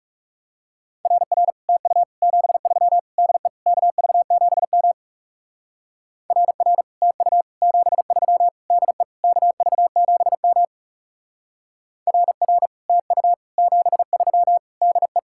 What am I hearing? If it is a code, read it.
Morse code, heard as a keyed tone, about 38 wpm, 700 Hz. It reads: RR TU 73 DE KV7M RR TU 73 DE KV7M RR TU 73 DE